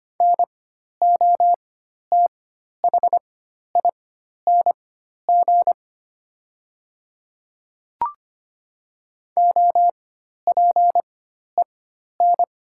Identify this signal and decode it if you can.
Morse code, heard as a keyed tone, about 25 wpm, spaced as for 12 wpm, 700 Hz. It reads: NOTHING OPEN
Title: 4x Character Spacing - Top 500 Words - 25wpm